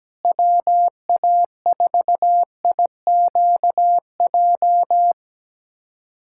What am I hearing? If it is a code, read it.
Morse code, heard as a keyed tone, about 17 wpm, 700 Hz. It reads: WA4IQJ
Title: USA Callsign DE Callsign - 6 Characters - 17wpm